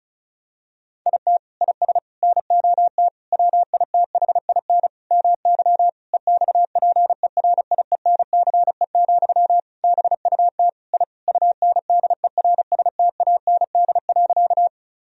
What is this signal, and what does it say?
Morse code, heard as a keyed tone, about 36 wpm, 700 Hz. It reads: IT IS NOT WITHIN MY EXPERIENCE, BUT I UNDERSTAND.
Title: Sentences from Top 1000 Words - Review Part 4 - 35wpm